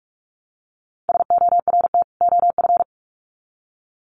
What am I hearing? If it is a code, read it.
Morse code, heard as a keyed tone, about 44 wpm, 700 Hz. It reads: SORT OF
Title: Sets of 2 Words 45wpm